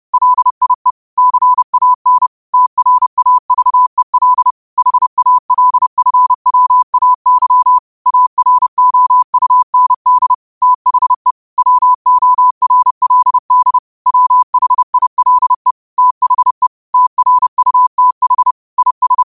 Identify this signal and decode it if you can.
Morse code, heard as a keyed tone, about 30 wpm, 1000 Hz. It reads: LIE CAN TRAVEL HALFWAY AROUND THE WORLD WHILE THE TRUTH IS